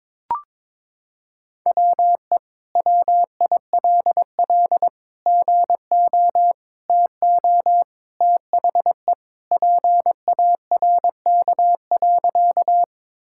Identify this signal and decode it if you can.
Morse code, heard as a keyed tone, about 22 wpm, 700 Hz. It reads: WE WILL GO TO THE PARK.
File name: practice / 22wpm / Sentences with 2-Word Phrases - 22wpm